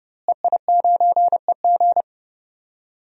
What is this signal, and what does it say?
Morse code, heard as a keyed tone, about 30 wpm, 700 Hz. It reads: EI9EG
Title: DXCC-100 Call Signs - 5 Characters - Repeated 3x - 30wpm